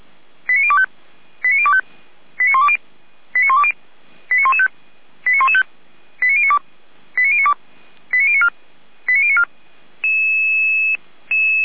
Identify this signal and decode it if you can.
Five-tone selective call (ZVEI), heard as quick runs of five tones, tones of about 80 ms. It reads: 89025 89025 89120 89120 89115 89115 89092 89092 89094 89094